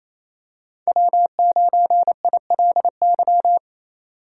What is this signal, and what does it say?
Morse code, heard as a keyed tone, about 28 wpm, 700 Hz. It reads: W9ILY
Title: K1USN SST QSO Practice - 28wpm